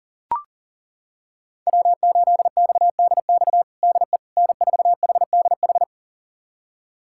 Morse code, heard as a keyed tone, about 40 wpm, 700 Hz. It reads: W8XDX DE N4HDH
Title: USA Callsign DE Callsign - 5 Characters - 40wpm